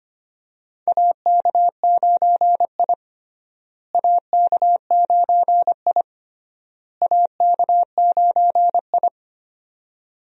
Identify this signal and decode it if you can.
Morse code, heard as a keyed tone, about 25 wpm, 700 Hz. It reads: AK9I AK9I AK9I
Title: USA Call Signs - 4 Characters - Repeated 3x - 25wpm